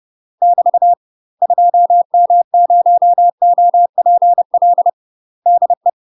Morse code, heard as a keyed tone, about 30 wpm, 700 Hz. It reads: <BT> 2M0OPL DE
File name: UK MASTER QSOs TRAINING / 30WPM / UK MASTER QSOs 0036